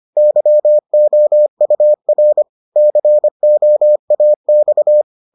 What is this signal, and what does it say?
Morse code, heard as a keyed tone, about 25 wpm, 600 Hz. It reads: YOUR COAX